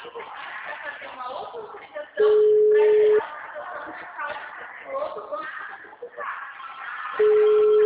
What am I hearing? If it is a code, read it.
Ringback tone (Europe, 425 Hz), with speech.
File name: out- -2048- - - .46979